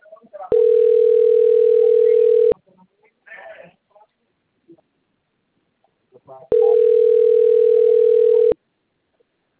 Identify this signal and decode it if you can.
Ringback tone (North America), 440 + 480 Hz.